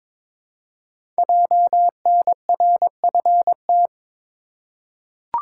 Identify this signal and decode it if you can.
Morse code, heard as a keyed tone, about 23 wpm, 700 Hz. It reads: JNRFT
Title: Code Groups - 5 Letter:Numbers - 22wpm